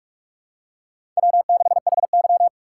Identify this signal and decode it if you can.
Morse code, heard as a keyed tone, about 46 wpm, 700 Hz. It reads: W6HY
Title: USA Call Signs - 4 Characters - 3x Character Spacing - 45wpm